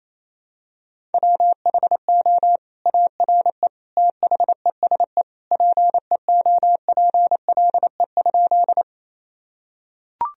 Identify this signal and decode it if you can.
Morse code, heard as a keyed tone, about 28 wpm, 700 Hz. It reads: WHO ARE THESE PEOPLE?
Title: Sentences from Top 1000 Words - Review Part 1 - 28wpm